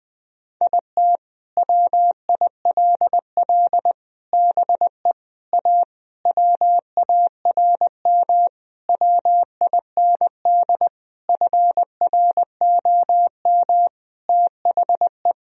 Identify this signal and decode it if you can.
Morse code, heard as a keyed tone, about 20 wpm, 700 Hz. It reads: IT WILL BE A WARM WIND FROM THE
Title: Sentences from Top 1000 Words - Review Part 3 - 20wpm